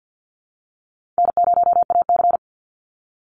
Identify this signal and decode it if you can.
Morse code, heard as a keyed tone, about 50 wpm, 700 Hz. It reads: N0AC
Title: Easy Call Signs - 50wpm